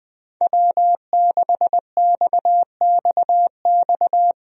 Morse code, heard as a keyed tone, about 21 wpm, 700 Hz. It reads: W6XXX
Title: USA Call Signs - 5 Characters - 20wpm